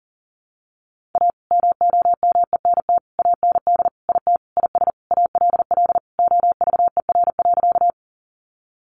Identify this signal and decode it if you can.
Morse code, heard as a keyed tone, about 37 wpm, 700 Hz. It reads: A MOMENT AND IT IS ALL OVER.
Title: Sentences from Top 1000 Words 40wpm